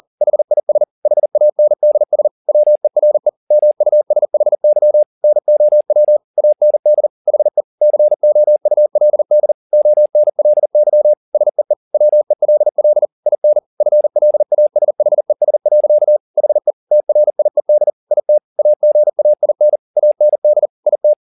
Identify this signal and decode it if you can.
Morse code, heard as a keyed tone, about 40 wpm, 600 Hz. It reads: HIS HANDS WERE MUSHY NOW AND HE COULD ONLY SEE WELL IN FLASHES. HE TRIED IT AGAIN AND IT
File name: the old man and the sea 2 40wpm